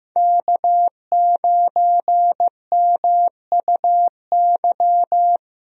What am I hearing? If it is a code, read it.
Morse code, heard as a keyed tone, about 16 wpm, 700 Hz. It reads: K9MUY
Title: USA Call Signs - 5 Characters - Repeated 3x - 15wpm